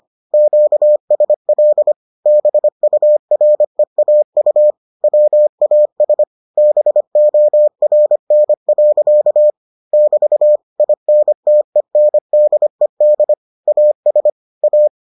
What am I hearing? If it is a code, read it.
Morse code, heard as a keyed tone, about 25 wpm, 600 Hz. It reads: QSL BUREAU WAS BORN. <BT> INTENDED AS A